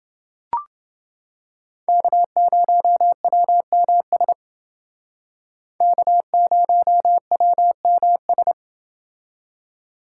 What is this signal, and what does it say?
Morse code, heard as a keyed tone, about 28 wpm, 700 Hz. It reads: K0WMS K0WMS